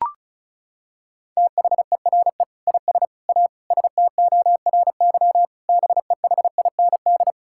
Morse code, heard as a keyed tone, about 36 wpm, 700 Hz. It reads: THERE IS A STORY BEHIND
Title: Sentences from Top 1000 Words - Review Part 5 - 35wpm